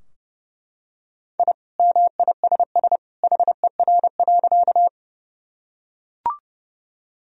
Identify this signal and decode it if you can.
Morse code, heard as a keyed tone, about 30 wpm, 700 Hz. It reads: I MISS HER.
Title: Sentences from Top 1000 Words - Review Part 3 - 30wpm